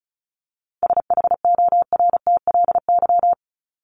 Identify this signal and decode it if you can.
Morse code, heard as a keyed tone, about 34 wpm, 700 Hz. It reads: SHORTLY
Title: LY Suffix - Top 100 Words with Base Words 35wpm